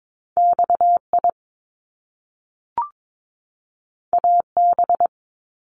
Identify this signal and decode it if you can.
Morse code, heard as a keyed tone, about 22 wpm, 700 Hz. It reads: XI AB
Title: Top Two Letter Combinations in Words 22wpm